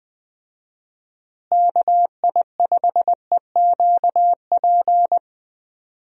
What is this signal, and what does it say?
Morse code, heard as a keyed tone, about 21 wpm, 700 Hz. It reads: KI5EQP
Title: USA Call Signs - 6 Characters - Repeated 3x - 20wpm